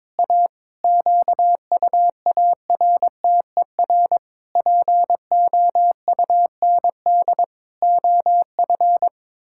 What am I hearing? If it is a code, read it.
Morse code, heard as a keyed tone, about 22 wpm, 700 Hz. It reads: A QUARTER POUND OF
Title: Pangram Challenge v2 - 22wpm